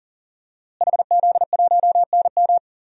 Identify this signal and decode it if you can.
Morse code, heard as a keyed tone, about 40 wpm, 700 Hz. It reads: HZ1NM
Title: DXCC-100 Call Signs - 5 Characters - Repeated 2x - 40wpm